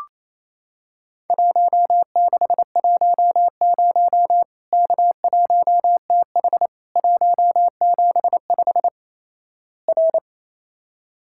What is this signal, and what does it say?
Morse code, heard as a keyed tone, about 27 wpm, 700 Hz. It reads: 1610 K1TH 175 R
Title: Worked All Europe DX Contest - QTC Exchange - 28wpm